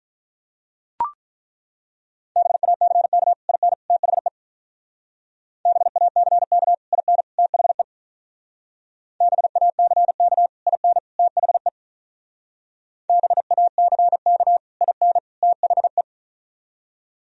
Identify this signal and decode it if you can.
Morse code, heard as a keyed tone, about 42 wpm, 700 Hz. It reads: BACK IN THE BACK IN THE BACK IN THE BACK IN THE
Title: Speed-Racing - Sets of 3 Words 35wpm